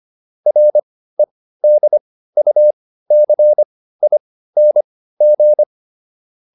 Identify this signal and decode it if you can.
Morse code, heard as a keyed tone, about 25 wpm, spaced as for 15 wpm, 600 Hz. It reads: REDUCING